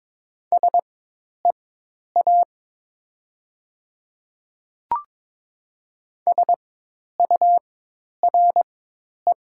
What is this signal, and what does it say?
Morse code, heard as a keyed tone, about 22 wpm, spaced as for 10 wpm, 700 Hz. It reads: SEA SURE
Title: 4x Character Spacing - Top 300 Words - 22wpm